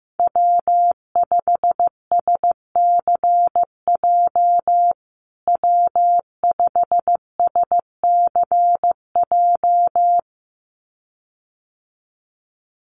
Morse code, heard as a keyed tone, about 15 wpm, 700 Hz. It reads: W5SCJ W5SCJ
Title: USA Call Signs Repeated 2x 15wpm